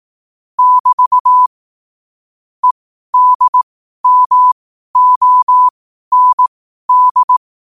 Morse code, heard as a keyed tone, about 18 wpm, spaced as for 12 wpm, 1000 Hz. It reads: <BT> EDMOND
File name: qotdfiles / 12wpm